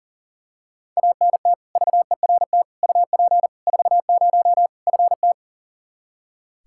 Morse code, heard as a keyed tone, about 40 wpm, 700 Hz. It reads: ANT VERT UP 40 FT
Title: Practice QSO - Exchange 2 - Info - 40wpm